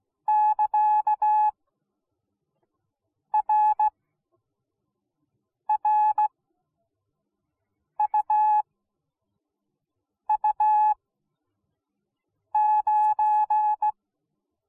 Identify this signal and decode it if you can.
Morse code, heard as a keyed tone, about 17 wpm, spaced as for 5 wpm, 850 Hz. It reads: <KA>RRUU9